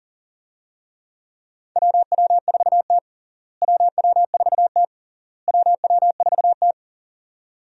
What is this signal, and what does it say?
Morse code, heard as a keyed tone, about 37 wpm, 700 Hz. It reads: WW4T WW4T WW4T
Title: USA Call Signs - 4 Characters - Repeated 3x - 40wpm